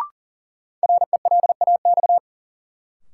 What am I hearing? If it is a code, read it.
Morse code, heard as a keyed tone, about 38 wpm, 700 Hz. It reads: RELAX
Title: Sentences from Top 1400 Words - New Word Drill Alt - 40wpm